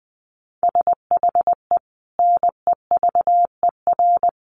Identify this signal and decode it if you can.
Morse code, heard as a keyed tone, about 20 wpm, 700 Hz. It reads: SHE NEVER
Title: Sentences from Top 300 Words 20wpm